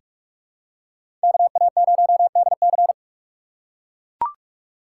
Morse code, heard as a keyed tone, about 42 wpm, 700 Hz. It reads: KA0DC
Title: USA Call Signs - 3x Character Spacing - 45wpm